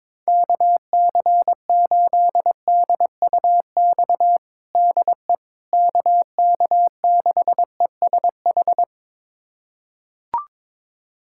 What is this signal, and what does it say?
Morse code, heard as a keyed tone, about 22 wpm, 700 Hz. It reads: KC8DUX DE KK6ESH
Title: USA Callsign DE Callsign 22wpm